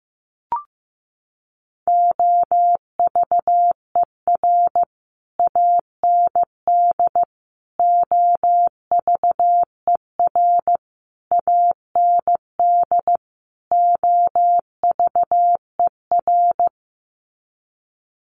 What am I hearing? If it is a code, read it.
Morse code, heard as a keyed tone, about 15 wpm, 700 Hz. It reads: OVER AND OVER AND OVER
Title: Sets of 5 Words Encore 15wpm